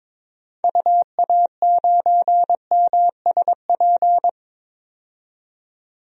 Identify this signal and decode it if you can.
Morse code, heard as a keyed tone, about 22 wpm, 700 Hz. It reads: UA9MSP
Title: DXCC-100 Call Signs - 6 Characters - 22wpm